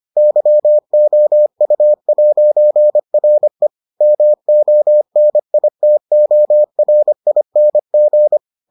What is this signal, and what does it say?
Morse code, heard as a keyed tone, about 25 wpm, 600 Hz. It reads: YOU'RE MONITORING